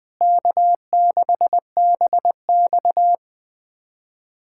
Morse code, heard as a keyed tone, about 20 wpm, 700 Hz. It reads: K6BX
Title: USA Callsign DE Callsign - 4 Characters - 20wpm